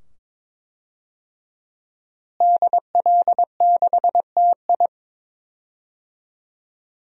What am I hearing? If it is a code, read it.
Morse code, heard as a keyed tone, about 21 wpm, 700 Hz. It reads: DL6TI
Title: DXCC-100 Call Signs - 5 Characters - Repeated 2x - 22wpm